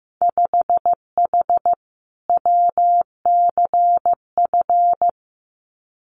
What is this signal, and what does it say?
Morse code, heard as a keyed tone, about 15 wpm, 700 Hz. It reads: 5H WCF
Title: Winter Field Day Exchanges Encore 15wpm